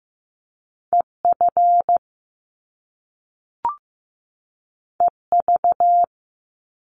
Morse code, heard as a keyed tone, about 15 wpm, 700 Hz. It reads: EF EV
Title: Top Two Letter Combinations in Words 15wpm